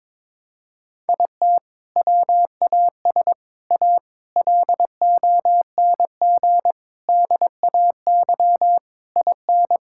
Morse code, heard as a keyed tone, about 22 wpm, 700 Hz. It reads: IT WAS A LONG DAY IN
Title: Sentences from Top 1600 Words 22wpm